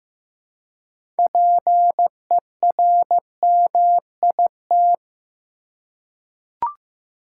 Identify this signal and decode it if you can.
Morse code, heard as a keyed tone, about 15 wpm, 700 Hz. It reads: PERMIT
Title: Six Letter Words - 15wpm